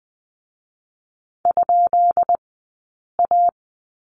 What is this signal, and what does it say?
Morse code, heard as a keyed tone, about 20 wpm, spaced as for 8 wpm, 700 Hz. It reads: ?A
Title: 144 Lesson 24d - Letter y - Two Characters 20wpm